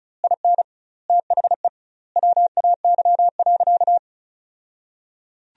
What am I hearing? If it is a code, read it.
Morse code, heard as a keyed tone, about 36 wpm, 700 Hz. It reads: IN THE WAY.
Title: 2x Word-Spacing - Sentences from Top 400 Words - 35wpm